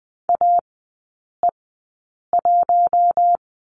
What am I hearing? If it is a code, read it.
Morse code, heard as a keyed tone, about 20 wpm, spaced as for 8 wpm, 700 Hz. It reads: AE1
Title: 074 Lesson 13e - Letter L - Three Characters 20wpm